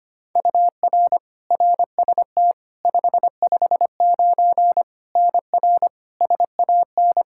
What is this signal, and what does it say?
Morse code, heard as a keyed tone, about 25 wpm, 700 Hz. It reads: UR RST 559 NR SAN
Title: Practice QSO - Exchange 1 - Beginning - 25wpm